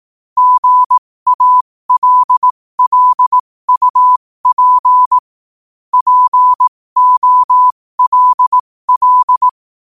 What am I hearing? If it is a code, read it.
Morse code, heard as a keyed tone, about 18 wpm, spaced as for 15 wpm, 1000 Hz. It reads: GALLUP POLL
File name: qotdfiles / 15wpm